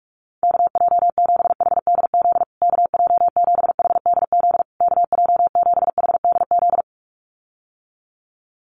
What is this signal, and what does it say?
Morse code, heard as a keyed tone, about 44 wpm, 700 Hz. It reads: KJ7HDZ KJ7HDZ KJ7HDZ
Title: USA Call Signs Repeated 3x 45wpm